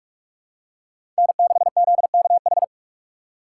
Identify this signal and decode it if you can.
Morse code, heard as a keyed tone, about 46 wpm, 700 Hz. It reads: N6ZKH
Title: USA Call Signs - 3x Character Spacing - 45wpm